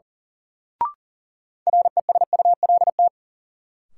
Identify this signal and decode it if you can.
Morse code, heard as a keyed tone, about 37 wpm, 700 Hz. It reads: RESULT
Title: Sets of 3 Words - New Word Drill - 40wpm